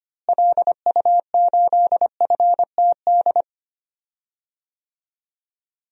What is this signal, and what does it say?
Morse code, heard as a keyed tone, about 25 wpm, 700 Hz. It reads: LU8FTD
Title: DXCC-100 Call Signs - 6 Characters - Repeated 2x - 25wpm